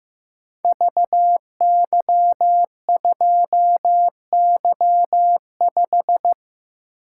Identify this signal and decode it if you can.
Morse code, heard as a keyed tone, about 15 wpm, 700 Hz. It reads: VY2Y5
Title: Code Groups - 5 Letter:Numbers - 15wpm